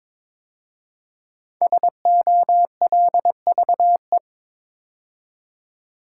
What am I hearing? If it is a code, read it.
Morse code, heard as a keyed tone, about 22 wpm, 700 Hz. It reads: SOLVE
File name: 5 Letter Words - First 3 Letters Given - 22wpm